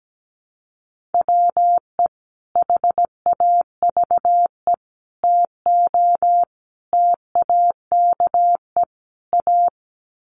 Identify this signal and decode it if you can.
Morse code, heard as a keyed tone, about 17 wpm, 700 Hz. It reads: WE HAVE TO TAKE A
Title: Sets of 5 Words Encore 17wpm